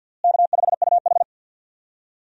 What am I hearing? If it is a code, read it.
Morse code, heard as a keyed tone, about 47 wpm, 700 Hz. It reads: K5UH
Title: USA Call Signs - 4 Characters - Repeated 2x - 50wpm